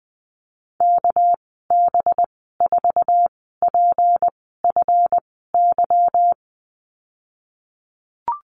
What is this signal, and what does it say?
Morse code, heard as a keyed tone, about 20 wpm, spaced as for 15 wpm, 700 Hz. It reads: KB4PFY
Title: 219 Lesson 35g - Number 0 - Call Signs 20wpm